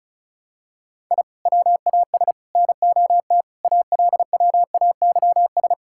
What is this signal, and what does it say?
Morse code, heard as a keyed tone, about 36 wpm, 700 Hz. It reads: I WAS NOT ALWAYS
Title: Sentences from Top 1000 Words - Review Part 6 - 35wpm